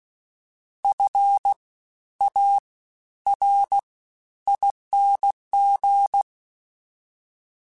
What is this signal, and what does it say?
Morse code, heard as a keyed tone, about 16 wpm, 800 Hz. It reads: F A R ING
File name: Special INGs-mp3s / Special INGs-mp3s / Fast ing @15